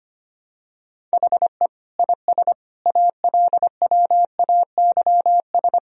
Morse code, heard as a keyed tone, about 25 wpm, 700 Hz. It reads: HE IS ALWAYS
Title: Sentences from Top 1000 Words - Review Part 3 - 25wpm